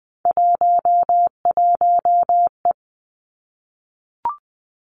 Morse code, heard as a keyed tone, about 19 wpm, 700 Hz. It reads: 11E
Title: Three Challenging Letter-Number Combinations 20wpm